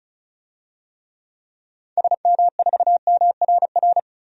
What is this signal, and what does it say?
Morse code, heard as a keyed tone, about 34 wpm, 700 Hz. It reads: SM4MRR
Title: DXCC-100 Call Signs - 6 Characters - Repeated 2x - 35wpm